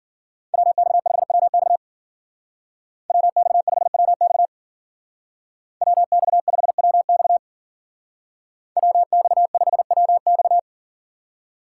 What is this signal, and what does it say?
Morse code, heard as a keyed tone, about 52 wpm, 700 Hz. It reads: WX5WX WX5WX WX5WX WX5WX